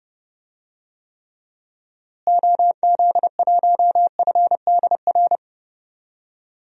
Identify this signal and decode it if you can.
Morse code, heard as a keyed tone, about 30 wpm, 700 Hz. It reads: OZ1FDR
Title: DXCC-100 Call Signs - 6 Characters - Repeated 2x - 30wpm